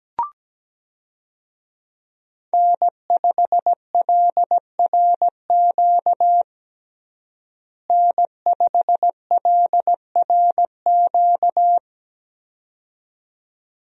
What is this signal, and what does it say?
Morse code, heard as a keyed tone, about 17 wpm, 700 Hz. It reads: N5LRQ N5LRQ
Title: USA Call Signs - 5 Characters - Repeated 2x - 17wpm